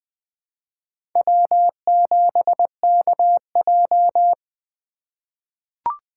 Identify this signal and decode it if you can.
Morse code, heard as a keyed tone, about 20 wpm, 700 Hz. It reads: W7KJ
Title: USA Call Signs - 4 Characters - 3x Character Spacing - 20wpm